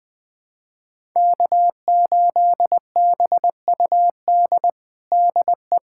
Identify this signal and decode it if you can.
Morse code, heard as a keyed tone, about 20 wpm, 700 Hz. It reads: K8BUD DE
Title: USA Callsign DE Callsign - 5 Characters - 20wpm